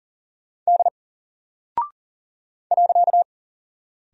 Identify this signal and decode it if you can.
Morse code, heard as a keyed tone, about 40 wpm, 700 Hz. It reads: D .
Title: Single Character (Letter:Number:Punctuation:Prosign) 40wpm